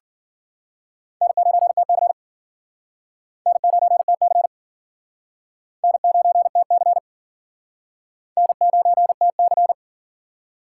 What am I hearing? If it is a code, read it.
Morse code, heard as a keyed tone, about 51 wpm, 700 Hz. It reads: N9TC N9TC N9TC N9TC